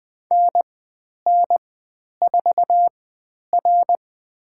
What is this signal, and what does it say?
Morse code, heard as a keyed tone, about 20 wpm, spaced as for 10 wpm, 700 Hz. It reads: NN4R
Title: 095 Lesson 16g - Letter c - Call Signs 20wpm